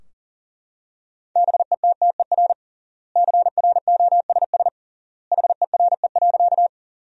Morse code, heard as a keyed tone, about 40 wpm, 700 Hz. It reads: BETTER CROSS HERE.
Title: 3x Word-Spacing - Sentences from Top 400 Words - 40wpm